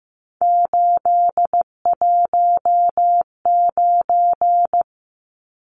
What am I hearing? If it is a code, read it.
Morse code, heard as a keyed tone, about 15 wpm, 700 Hz. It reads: 819
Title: Numbers - 3-Digit 15wpm